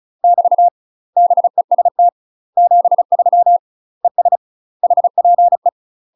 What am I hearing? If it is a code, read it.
Morse code, heard as a keyed tone, about 36 wpm, 700 Hz. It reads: <BT> BEST 73 ES HPE
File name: UK MASTER QSOs TRAINING / 35WPM / UK MASTER QSOs 0002